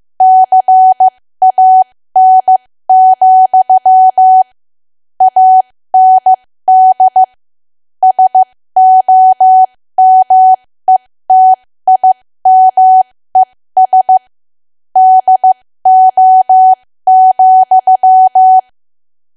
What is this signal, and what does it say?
Morse code, heard as a keyed tone, about 15 wpm, spaced as for 13 wpm, 750 Hz. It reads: CAN, AND SOMETIMES DO,
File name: cwa / W1AW-Level2 / 13WPM